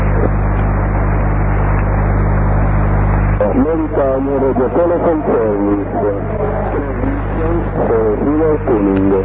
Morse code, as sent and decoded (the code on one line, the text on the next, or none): none